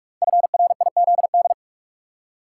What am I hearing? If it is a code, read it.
Morse code, heard as a keyed tone, about 44 wpm, 700 Hz. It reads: FRIZD
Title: Code Groups - 5 Letter:Numbers - 45wpm